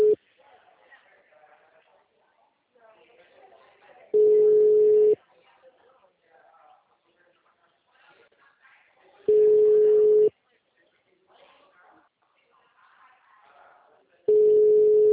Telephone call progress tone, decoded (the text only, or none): ringback tone